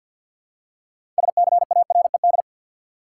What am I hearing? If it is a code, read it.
Morse code, heard as a keyed tone, about 50 wpm, 700 Hz. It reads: SCARED
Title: Six Letter Words - 50wpm